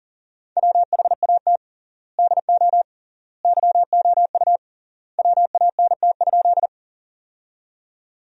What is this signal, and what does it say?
Morse code, heard as a keyed tone, about 37 wpm, 700 Hz. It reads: WHAT DO YOU WANT?
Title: 3x Word-Spacing - Sentences from Top 200 Words - 40wpm